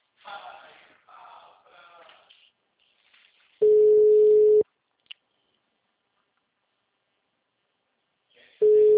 Ringback tone (Europe, 425 Hz).
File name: out- -2035- - - .98